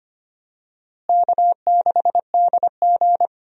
Morse code, heard as a keyed tone, about 25 wpm, 700 Hz. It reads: K6DG
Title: USA Call Signs - 4 Characters - Repeated 3x - 25wpm